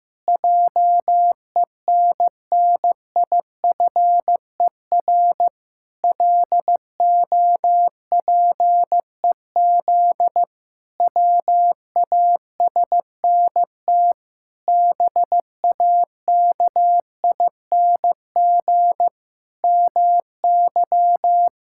Morse code, heard as a keyed tone, about 16 wpm, 700 Hz. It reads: JENNIFER LOPEZ WASNT BAKING MY
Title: Pangram Challenge v2 - 15wpm